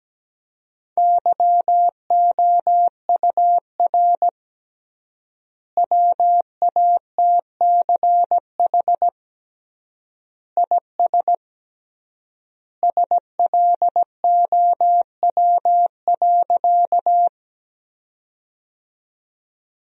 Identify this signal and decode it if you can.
Morse code, heard as a keyed tone, about 17 wpm, 700 Hz. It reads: YOUR WATCH IS SLOW.
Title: 3x Word-Spacing - Sentences from Top 400 Words - 17wpm